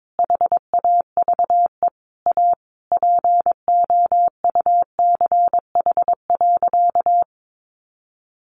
Morse code, heard as a keyed tone, about 22 wpm, 700 Hz. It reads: HAVE A POUCH.
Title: Easy Phrases 22wpm